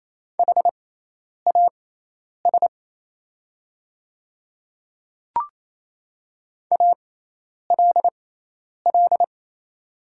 Morse code, heard as a keyed tone, about 27 wpm, spaced as for 10 wpm, 690 Hz. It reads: HAS ALL
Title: 6x Character Spacing - Top 100 Words - 28wpm